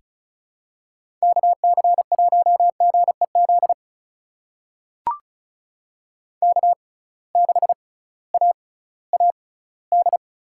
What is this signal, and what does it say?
Morse code, heard as a keyed tone, about 34 wpm, spaced as for 19 wpm, 700 Hz. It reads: KC1GEZ K6AAD
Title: USA Call Signs - 6x Character Spacing - 35wpm